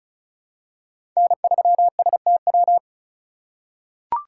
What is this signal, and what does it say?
Morse code, heard as a keyed tone, about 36 wpm, 700 Hz. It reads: N3STW
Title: USA Call Signs - 5 Characters - 2x Character Spacing - 35wpm